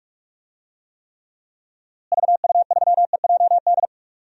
Morse code, heard as a keyed tone, about 44 wpm, 700 Hz. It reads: VU3EJD
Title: DXCC-100 Call Signs - 6 Characters - Repeated 3x - 45wpm